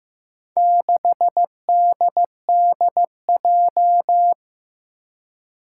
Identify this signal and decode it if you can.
Morse code, heard as a keyed tone, about 15 wpm, 700 Hz. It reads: 6DDJ
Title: Four Challenging Letter-Number Combinations 15wpm